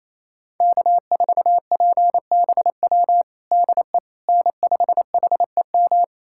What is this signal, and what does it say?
Morse code, heard as a keyed tone, about 27 wpm, 700 Hz. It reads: K4PBW DE N5HEM
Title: USA Callsign DE Callsign - 5 Characters - 28wpm